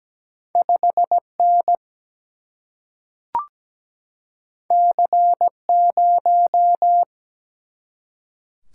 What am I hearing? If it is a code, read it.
Morse code, heard as a keyed tone, about 17 wpm, 700 Hz. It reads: 5 N C 0
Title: Single-Letter Single-Number 17wpm